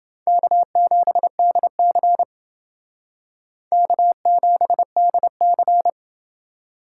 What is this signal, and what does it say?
Morse code, heard as a keyed tone, about 28 wpm, 700 Hz. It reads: K7DC K7DC